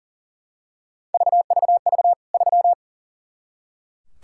Morse code, heard as a keyed tone, about 40 wpm, 700 Hz. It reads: VVV 3
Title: K1USN SST QSO Practice 40wpm